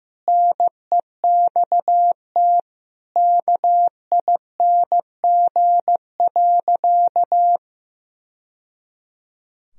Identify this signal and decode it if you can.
Morse code, heard as a keyed tone, about 16 wpm, 700 Hz. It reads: NEXT KING.
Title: Sentences from Top 1000 Words - Review Part 3 - 15wpm